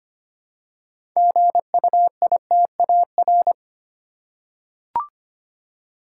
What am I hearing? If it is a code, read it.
Morse code, heard as a keyed tone, about 25 wpm, 700 Hz. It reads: GUITAR
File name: practice / 25wpm / Easy Words with Hints - 25wpm